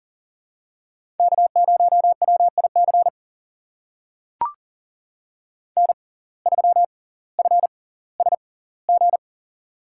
Morse code, heard as a keyed tone, about 40 wpm, spaced as for 20 wpm, 700 Hz. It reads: K0WIC N3FSG
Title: USA Call Signs - 5 Characters - 6x Character Spacing - 40wpm